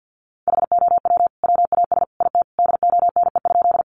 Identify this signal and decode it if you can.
Morse code, heard as a keyed tone, about 50 wpm, 700 Hz. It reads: HOW WAS IT DONE?
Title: Sentences from Top 500 Words 50wpm